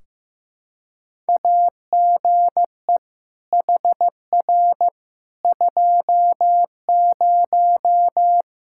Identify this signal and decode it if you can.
Morse code, heard as a keyed tone, about 15 wpm, 700 Hz. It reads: AGE HR 20
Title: Practice QSO - Exchange 3 - More Info - 15wpm